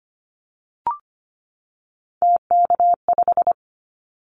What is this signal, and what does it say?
Morse code, heard as a keyed tone, about 25 wpm, 700 Hz. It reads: TK5
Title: Callsign Prefixes for Top 100 DXCC - 25wpm